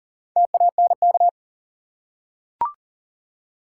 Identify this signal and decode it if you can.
Morse code, heard as a keyed tone, about 40 wpm, 700 Hz. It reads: TANK
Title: Four Letter Words - 40wpm